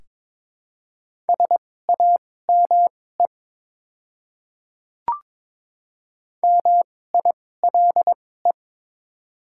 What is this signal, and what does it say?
Morse code, heard as a keyed tone, about 22 wpm, spaced as for 16 wpm, 700 Hz. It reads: SAME MILE
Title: 2x Character Spacing - Top 300 Words - 22wpm